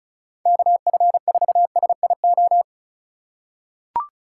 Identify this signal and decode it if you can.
Morse code, heard as a keyed tone, about 36 wpm, 700 Hz. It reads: KF4SIO
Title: USA Call Signs - 3x Character Spacing - 35wpm